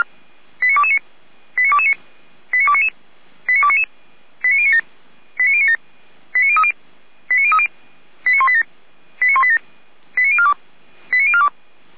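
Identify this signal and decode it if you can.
Five-tone selective call (ZVEI), heard as quick runs of five tones, tones of about 70 ms. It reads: 89229 89229 89299 89299 89097 89097 89030 89030 89187 89187 89042 89042
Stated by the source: ZVEI five-tone selective call received by SDR, from the Feuerwehr Orlamünde audio server.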